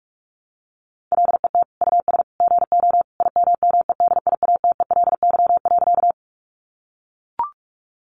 Morse code, heard as a keyed tone, about 44 wpm, 700 Hz. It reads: LET US GO IMMEDIATELY.
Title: Sentences from Top 1100 Words 45wpm